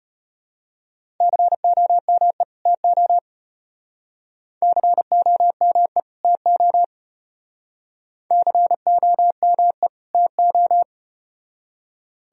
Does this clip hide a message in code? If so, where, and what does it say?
Yes: Morse code from 1.2 to 10.8 s, heard as a keyed tone, about 33 wpm, 700 Hz. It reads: COME TO COME TO COME TO